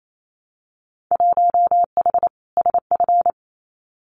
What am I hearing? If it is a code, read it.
Morse code, heard as a keyed tone, about 28 wpm, 700 Hz. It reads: 1H SF
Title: Winter Field Day Exchanges 28wpm